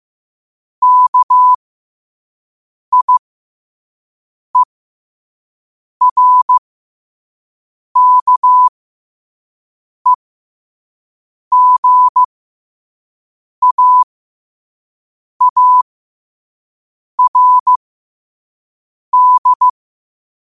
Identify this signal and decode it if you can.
Morse code, heard as a keyed tone, about 15 wpm, spaced as for 5 wpm, 1000 Hz. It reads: KIERKEGAARD
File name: qotdfiles / 5wpm